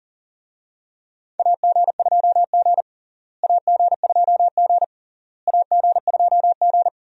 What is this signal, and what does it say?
Morse code, heard as a keyed tone, about 40 wpm, 700 Hz. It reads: AG2G AG2G AG2G
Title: USA Call Signs - 4 Characters - Repeated 3x - 40wpm